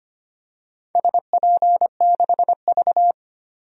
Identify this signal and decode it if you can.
Morse code, heard as a keyed tone, about 25 wpm, 700 Hz. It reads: SP6V
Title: DXCC-100 Call Signs - 4 Characters - Repeated 3x - 25wpm